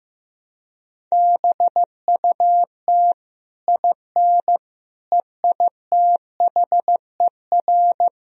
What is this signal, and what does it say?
Morse code, heard as a keyed tone, about 16 wpm, 700 Hz. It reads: BUT IN EITHER
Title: Sentences from Top 1000 Words - Review Part 4 - 15wpm